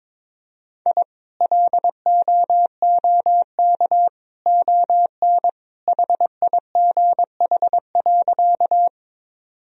Morse code, heard as a keyed tone, about 22 wpm, 700 Hz. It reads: I LOOK ON HIGH.
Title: Sentences from Top 1000 Words - Review Part 1 - 22wpm